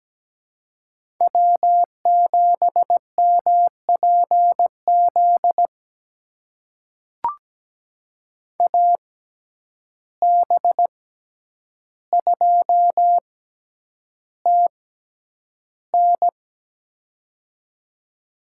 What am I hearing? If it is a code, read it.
Morse code, heard as a keyed tone, about 17 wpm, spaced as for 9 wpm, 700 Hz. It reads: W7MPZ AB2TN
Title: USA Call Signs - 5 Characters - 6x Character Spacing - 17wpm